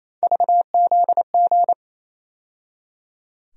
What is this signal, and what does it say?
Morse code, heard as a keyed tone, about 28 wpm, 700 Hz. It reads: VZG
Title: USA Call Sign Postfixes - 3 Letters 28wpm